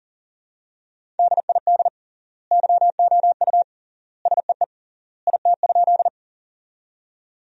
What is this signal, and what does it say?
Morse code, heard as a keyed tone, about 40 wpm, 700 Hz. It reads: DID YOU SEE IT?
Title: 3x Word-Spacing - Sentences from Top 100 Words - 40wpm